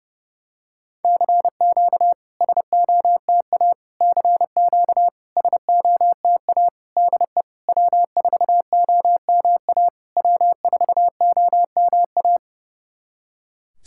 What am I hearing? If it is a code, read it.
Morse code, heard as a keyed tone, about 29 wpm, 690 Hz. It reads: CQ SOTA CQ SOTA DE W4OMA W4OMA
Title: SOTA QSO Practice 30wpm